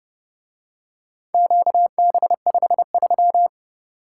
Morse code, heard as a keyed tone, about 30 wpm, 700 Hz. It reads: QB53